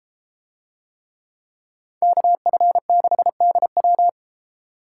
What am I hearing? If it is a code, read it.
Morse code, heard as a keyed tone, about 30 wpm, 700 Hz. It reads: KF6DW